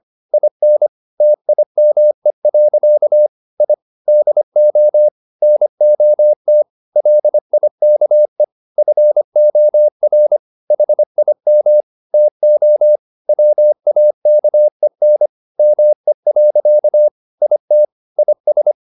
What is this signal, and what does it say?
Morse code, heard as a keyed tone, about 25 wpm, 600 Hz. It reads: IN TIME. I DO NOT LIKE FOR HIM TO WAKEN ME. IT IS
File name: the old man and the sea part 1 25wpm